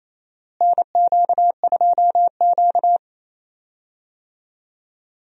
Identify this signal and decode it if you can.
Morse code, heard as a keyed tone, about 27 wpm, 700 Hz. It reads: NQ2Q
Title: USA Call Signs - 4 Characters - Repeated 3x - 28wpm